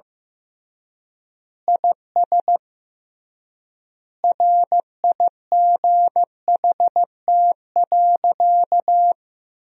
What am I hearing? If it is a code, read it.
Morse code, heard as a keyed tone, about 16 wpm, 700 Hz. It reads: IS RIGHT.
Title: 3x Word-Spacing - Sentences from Top 300 Words - 15wpm